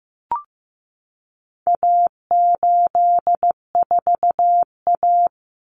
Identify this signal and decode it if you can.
Morse code, heard as a keyed tone, about 15 wpm, 700 Hz. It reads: A84A
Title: Common Cut Numbers - 4-Digit 15wpm